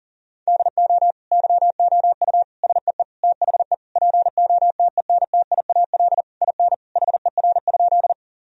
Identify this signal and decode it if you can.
Morse code, heard as a keyed tone, about 37 wpm, 700 Hz. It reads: DO YOU SEE THE POTENTIAL IN HER?
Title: Sentences from Top 1000 Words - Review Part 4 - 40wpm